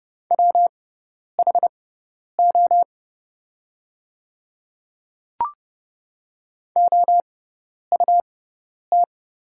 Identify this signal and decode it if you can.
Morse code, heard as a keyed tone, about 30 wpm, spaced as for 10 wpm, 700 Hz. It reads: WHO OUT
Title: 6x Character Spacing - Top 300 Words - 30wpm